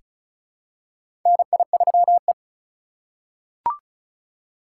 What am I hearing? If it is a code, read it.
Morse code, heard as a keyed tone, about 34 wpm, 700 Hz. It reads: NI3E
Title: USA Call Signs - 4 Characters - 2x Character Spacing - 35wpm